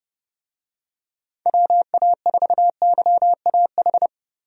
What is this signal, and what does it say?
Morse code, heard as a keyed tone, about 30 wpm, 700 Hz. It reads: WA4YAH
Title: USA Call Signs - 6 Characters - Repeated 2x - 30wpm